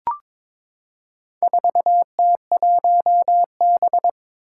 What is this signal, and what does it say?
Morse code, heard as a keyed tone, about 22 wpm, 700 Hz. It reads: 4T1B
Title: Four Challenging Letter-Number Combinations 22wpm